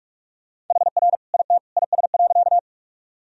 Morse code, heard as a keyed tone, about 42 wpm, 700 Hz. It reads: HR IT IS.
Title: Sentences from Top 200 Words and Abbreviations 45wpm